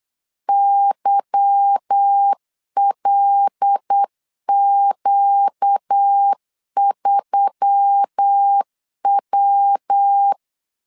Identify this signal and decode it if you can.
Morse code, heard as a keyed tone, about 8 wpm, 800 Hz. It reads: YLQ3W
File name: LEZIONI / 8 YJ45 / 8 Y 8wpm 3245